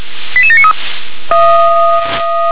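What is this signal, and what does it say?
Five-tone selective call (ZVEI), heard as quick runs of five tones, tones of about 70 ms. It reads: 88793
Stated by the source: ZVEI five-tone selective call 88692, received on an SDR.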